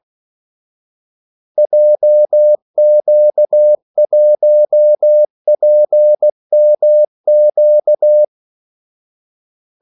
Morse code, heard as a keyed tone, about 16 wpm, 600 Hz. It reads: JQ1PMQ
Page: Kaldesignaler 16 wpm